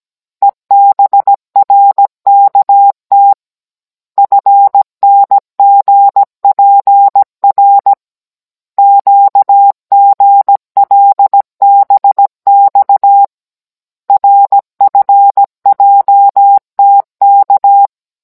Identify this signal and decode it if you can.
Morse code, heard as a keyed tone, about 17 wpm, 800 Hz. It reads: EBRKT FNGPR QGLBX RFJTK
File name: Cours-NR18-16mm